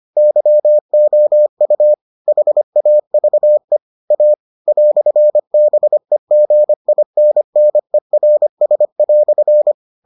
Morse code, heard as a keyed tone, about 25 wpm, 600 Hz. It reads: YOU HAVE A "BEGINNERS"